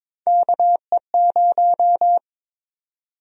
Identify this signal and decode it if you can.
Morse code, heard as a keyed tone, about 22 wpm, 700 Hz. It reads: KE0
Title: USA Call Sign Prefixes - Repeated 2x - 22wpm